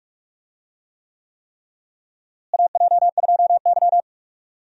Morse code, heard as a keyed tone, about 43 wpm, 700 Hz. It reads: AJ2Y
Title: USA Call Signs - 4 Characters - Repeated 3x - 45wpm